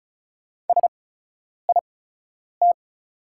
Morse code, heard as a keyed tone, about 36 wpm, spaced as for 10 wpm, 700 Hz. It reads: SIT
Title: 8x Character Spacing - Top 500 Words - 35wpm